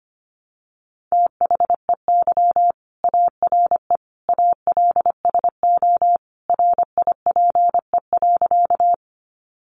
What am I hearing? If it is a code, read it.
Morse code, heard as a keyed tone, about 25 wpm, 700 Hz. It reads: THEY ARE ALSO RIPE.
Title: Additional Easy Phrases 25wpm